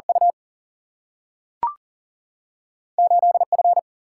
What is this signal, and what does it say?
Morse code, heard as a keyed tone, about 40 wpm, 700 Hz. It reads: U 8F
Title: Single-Letter Single-Number 40wpm